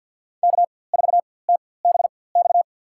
Morse code, heard as a keyed tone, about 47 wpm, spaced as for 23 wpm, 700 Hz. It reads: K4TBX
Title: USA Call Signs - 4x Character Spacing - 50wpm